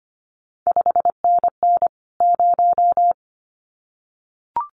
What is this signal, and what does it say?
Morse code, heard as a keyed tone, about 25 wpm, 700 Hz. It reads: 5NN 0
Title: CQ WW DX Contest Exchange Encore - 25wpm